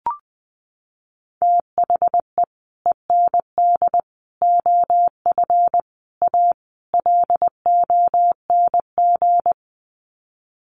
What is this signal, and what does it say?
Morse code, heard as a keyed tone, about 19 wpm, 700 Hz. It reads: THE END OF A LONG
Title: Sets of 5 Words Encore2 20wpm